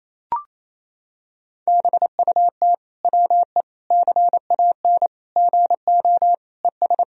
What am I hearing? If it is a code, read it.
Morse code, heard as a keyed tone, about 28 wpm, 700 Hz. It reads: BUT WE CAN GO ES
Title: Sentences from Top 100 Words and Abbreviations 28wpm